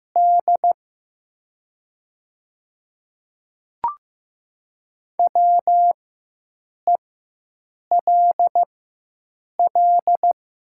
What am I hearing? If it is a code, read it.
Morse code, heard as a keyed tone, about 15 wpm, spaced as for 7 wpm, 700 Hz. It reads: D WELL
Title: 4x Character Spacing - Top 200 Words - 15wpm